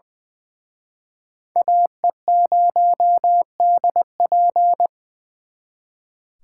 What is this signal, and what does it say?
Morse code, heard as a keyed tone, about 23 wpm, 700 Hz. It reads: AE0DP
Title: Speed Racing - USA 2x2 Call Signs 20wpm